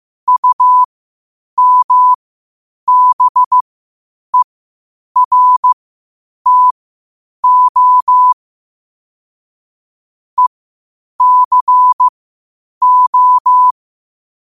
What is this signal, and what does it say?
Morse code, heard as a keyed tone, about 15 wpm, spaced as for 8 wpm, 1000 Hz. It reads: UMBERTO ECO
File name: qotdfiles / 8wpm